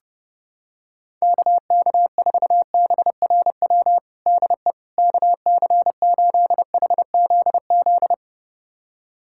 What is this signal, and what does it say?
Morse code, heard as a keyed tone, about 29 wpm, 700 Hz. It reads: KK4BRW DE KC8HZZ
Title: USA Callsign DE Callsign - 6 Characters - 30wpm